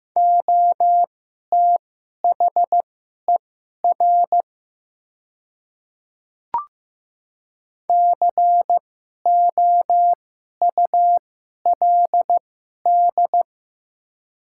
Morse code, heard as a keyed tone, about 15 wpm, spaced as for 11 wpm, 700 Hz. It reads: OTHER COULD
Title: 2x Character Spacing - Top 100 Words - 15wpm